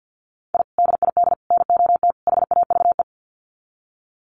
Morse code, heard as a keyed tone, about 50 wpm, 700 Hz. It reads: I DID NOT HAVE
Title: Sets of 4 Words 50wpm